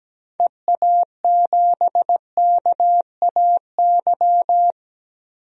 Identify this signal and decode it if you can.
Morse code, heard as a keyed tone, about 17 wpm, 700 Hz. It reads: EA7KAY
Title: DXCC-100 Call Signs - 6 Characters - 17wpm